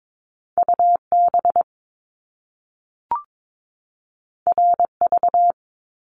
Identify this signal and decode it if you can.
Morse code, heard as a keyed tone, about 22 wpm, 700 Hz. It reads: UB RV
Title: Top Two Letter Combinations in Words 22wpm